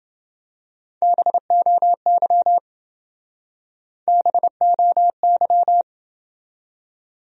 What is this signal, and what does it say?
Morse code, heard as a keyed tone, about 28 wpm, 700 Hz. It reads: BOY BOY